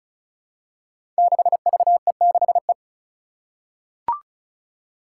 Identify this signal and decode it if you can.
Morse code, heard as a keyed tone, about 36 wpm, 700 Hz. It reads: 6VE6E
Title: Five Challenging Letter-Number Combinations 35wpm